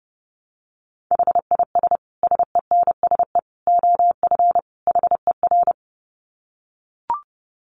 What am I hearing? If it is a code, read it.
Morse code, heard as a keyed tone, about 28 wpm, 700 Hz. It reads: HIS SENSE OF HER
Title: Dit Challenge 30wpm